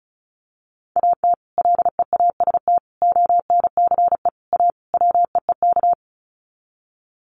Morse code, heard as a keyed tone, about 34 wpm, 700 Hz. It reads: AT LEAST ONCE A WEEK
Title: Sets of 5 Words Encore2 35wpm